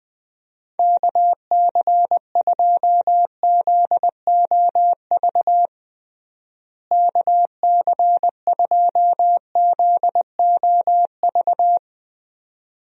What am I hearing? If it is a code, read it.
Morse code, heard as a keyed tone, about 20 wpm, 700 Hz. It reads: KC2ZOV KC2ZOV
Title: USA Call Signs - 6 Characters - Repeated 2x - 20wpm